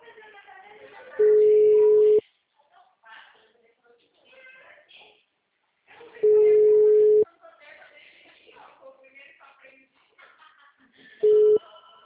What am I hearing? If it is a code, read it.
Ringback tone (Europe, 425 Hz), with speech.